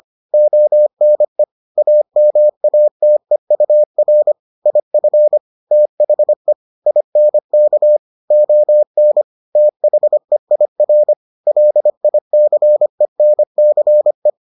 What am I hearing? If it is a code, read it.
Morse code, heard as a keyed tone, about 25 wpm, 600 Hz. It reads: ONE AMATEUR IF THE INK ON THEIR LICENCE